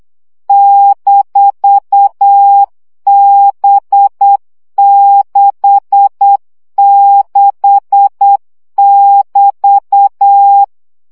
Morse code, heard as a keyed tone, about 8 wpm, 800 Hz. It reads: -B66<BT>
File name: CW gyakorlas b 6 egyenlo minusz 7 baud